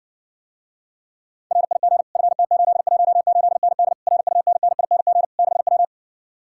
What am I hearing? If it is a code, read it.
Morse code, heard as a keyed tone, about 60 wpm, 700 Hz. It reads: RIG FT818ND RUNNING 6W